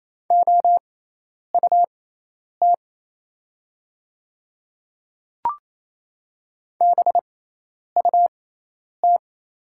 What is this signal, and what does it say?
Morse code, heard as a keyed tone, about 27 wpm, spaced as for 10 wpm, 690 Hz. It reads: OUT BUT
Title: 6x Character Spacing - Top 100 Words - 28wpm